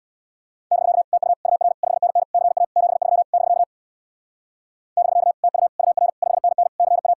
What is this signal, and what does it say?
Morse code, heard as a keyed tone, about 66 wpm, 700 Hz. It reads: <BK> TU UR 5NN CA 73 <BK> <BK> TU UR 5NN CA